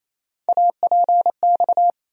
Morse code, heard as a keyed tone, about 27 wpm, 700 Hz. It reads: APX
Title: USA Call Sign Postfixes - 3 Letters 28wpm